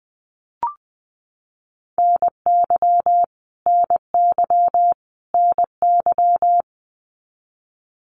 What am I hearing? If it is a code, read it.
Morse code, heard as a keyed tone, about 20 wpm, 700 Hz. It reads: NY NY NY
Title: US States Repeated 3x 20wpm